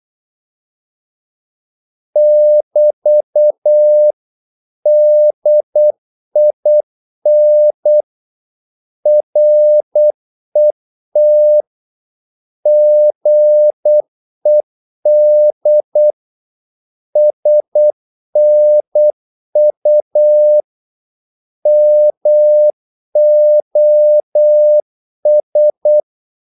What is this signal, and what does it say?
Morse code, heard as a keyed tone, about 8 wpm, 600 Hz. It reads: <BT> DIN RET GED SNU MOS
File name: Korte ord DK 8wpm